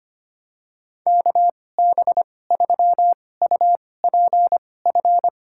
Morse code, heard as a keyed tone, about 25 wpm, spaced as for 18 wpm, 700 Hz. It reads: KB3UPF
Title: USA Call Signs - 6 Characters - 2x Character Spacing - 25wpm